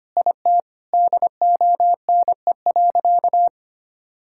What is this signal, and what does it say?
Morse code, heard as a keyed tone, about 25 wpm, 700 Hz. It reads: IT DONE.
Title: Sentences from Top 1000 Words - Review Part 5 - 25wpm